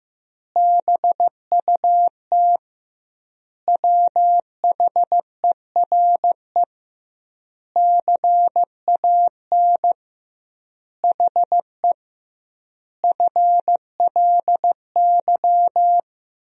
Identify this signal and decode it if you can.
Morse code, heard as a keyed tone, about 15 wpm, 700 Hz. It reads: BUT WHERE CAN HE FLY
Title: 2x Word-Spacing - Sentences from Top 500 Words - 15wpm